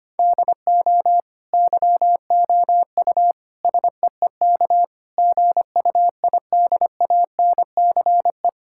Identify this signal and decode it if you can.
Morse code, heard as a keyed tone, about 25 wpm, 700 Hz. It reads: DO YOU SEEK GUIDANCE
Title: Sentences from Top 1600 Words 25wpm